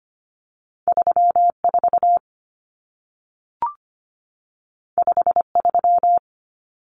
Morse code, heard as a keyed tone, about 25 wpm, 700 Hz. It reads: 34 53
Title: Numbers - 2-Digit 25wpm